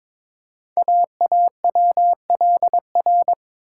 Morse code, heard as a keyed tone, about 22 wpm, 700 Hz. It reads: AAWLR
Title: Code Groups - 5 Letters - 22wpm